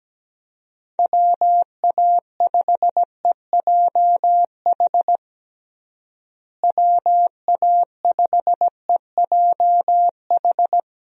Morse code, heard as a keyed tone, about 17 wpm, 700 Hz. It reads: WA5EJH WA5EJH
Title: USA Call Signs - 6 Characters - Repeated 2x - 17wpm